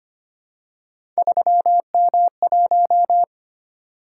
Morse code, heard as a keyed tone, about 25 wpm, 700 Hz. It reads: 3M1
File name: Three Letter-Number Combinations - 25wpm